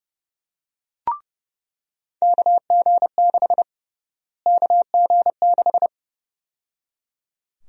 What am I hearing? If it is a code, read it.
Morse code, heard as a keyed tone, about 32 wpm, 700 Hz. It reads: KG6 KG6
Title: USA Call Sign Prefixes - Repeated 2x - 30wpm